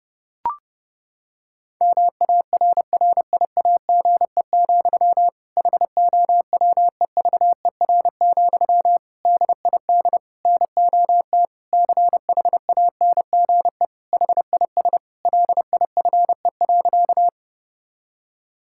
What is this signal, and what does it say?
Morse code, heard as a keyed tone, about 30 wpm, 700 Hz. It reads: MARRIAGE, HOWEVER, DID NOT CHANGE HIS LIFE.
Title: Sentences from Top 1000 Words - Review Part 5 - 30wpm